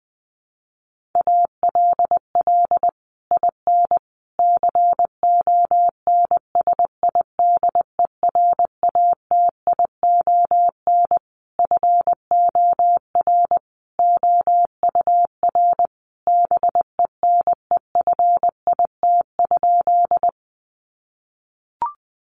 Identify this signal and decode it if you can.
Morse code, heard as a keyed tone, about 20 wpm, 700 Hz. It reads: ALL IN CONSIDERATION FOR OUR BENEFIT?
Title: Sentences from Top 1500 Words 20wpm